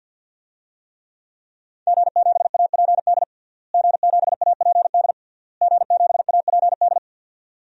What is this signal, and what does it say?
Morse code, heard as a keyed tone, about 48 wpm, 700 Hz. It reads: G7APD G7APD G7APD
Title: DXCC-100 Call Signs - 5 Characters - Repeated 3x - 50wpm